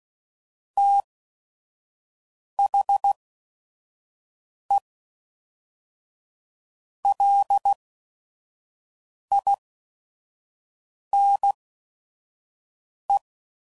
Morse code, heard as a keyed tone, about 16 wpm, spaced as for 5 wpm, 800 Hz. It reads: THE LINE
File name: CW Code Groups @5WPM